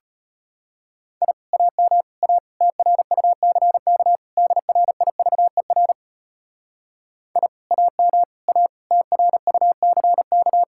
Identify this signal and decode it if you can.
Morse code, heard as a keyed tone, about 32 wpm, 700 Hz. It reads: I AM A TRUCK DRIVER I AM A TRUCK